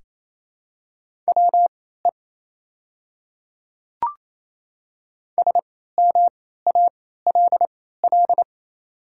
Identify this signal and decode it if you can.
Morse code, heard as a keyed tone, about 27 wpm, spaced as for 16 wpm, 700 Hz. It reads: WE SMALL
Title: 3x Character Spacing - Top 200 Words - 28wpm